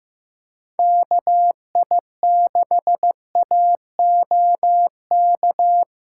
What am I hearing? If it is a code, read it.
Morse code, heard as a keyed tone, about 17 wpm, 700 Hz. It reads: KI6AOK
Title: Speed Racing - USA 2x3 Call Signs 15wpm